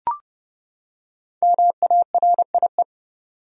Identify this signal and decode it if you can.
Morse code, heard as a keyed tone, about 30 wpm, 700 Hz. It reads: MARIE
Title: USA Common Names - 30wpm